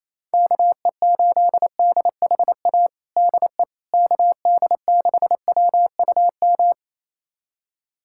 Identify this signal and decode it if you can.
Morse code, heard as a keyed tone, about 27 wpm, 700 Hz. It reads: KE8DHA DE KD6WUM
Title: USA Callsign DE Callsign - 6 Characters - 28wpm